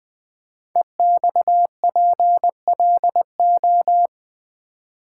Morse code, heard as a keyed tone, about 20 wpm, 700 Hz. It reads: EXPLO
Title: Build A Word - 5 Letters At A Time - 12-Letter Words - 20wpm